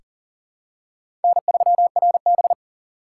Morse code, heard as a keyed tone, about 39 wpm, 700 Hz. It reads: N3RB
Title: USA Call Signs - 4 Characters - 3x Character Spacing - 40wpm